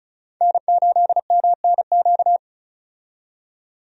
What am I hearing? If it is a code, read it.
Morse code, heard as a keyed tone, about 36 wpm, 700 Hz. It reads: N8MNQ
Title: USA Call Signs - 5 Characters - 35wpm